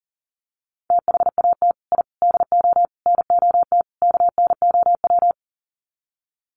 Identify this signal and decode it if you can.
Morse code, heard as a keyed tone, about 40 wpm, 700 Hz. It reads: THAT I DO NOT KNOW
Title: Sets of 5 Words Encore2 40wpm